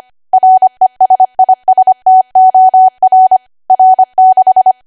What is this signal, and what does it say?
Morse code, heard as a keyed tone, about 25 wpm, 750 Hz. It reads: RESISTOR R6
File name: Archive / 25 WPM / 25WPM